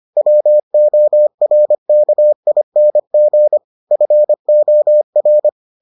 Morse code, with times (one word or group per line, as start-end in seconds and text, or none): 0.2-3.6 WORKING
3.9-5.5 FOR